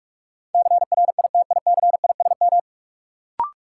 Morse code, heard as a keyed tone, about 42 wpm, 700 Hz. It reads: CRITICISM
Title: Sentences from Top 1300 Words - New Word Drill - 45wpm